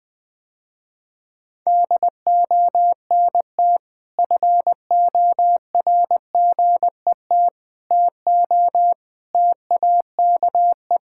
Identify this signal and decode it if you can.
Morse code, heard as a keyed tone, about 20 wpm, 700 Hz. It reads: DONT FORGET TO TAKE
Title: Sentences with 5-Word Phrases - Spoken Beginning - 20wpm